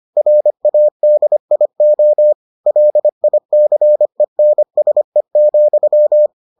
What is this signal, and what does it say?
Morse code, heard as a keyed tone, about 25 wpm, 600 Hz. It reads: RADIO LICENSE,